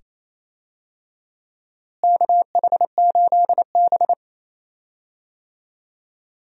Morse code, heard as a keyed tone, about 28 wpm, 700 Hz. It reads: KH8B
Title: USA Call Signs - 4 Characters - Repeated 2x - 28wpm